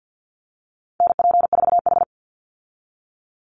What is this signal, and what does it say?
Morse code, heard as a keyed tone, about 50 wpm, 700 Hz. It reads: NP4H
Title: Call Signs Repeated 2x - 50wpm